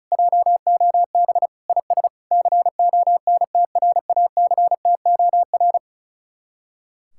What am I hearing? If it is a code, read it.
Morse code, heard as a keyed tone, about 43 wpm, 700 Hz. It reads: JOB IS CONTRACTOR
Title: Speed-Racing Common QSO Phrases - 35wpm